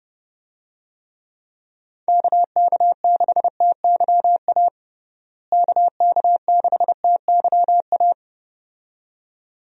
Morse code, heard as a keyed tone, about 30 wpm, 700 Hz. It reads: KK6TYA KK6TYA
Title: USA Call Signs - 6 Characters - Repeated 2x - 30wpm